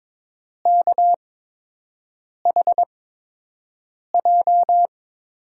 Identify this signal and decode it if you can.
Morse code, heard as a keyed tone, about 22 wpm, spaced as for 6 wpm, 700 Hz. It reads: KHJ
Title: USA Call Signs - 6 Characters - 8x Character Spacing - 22wpm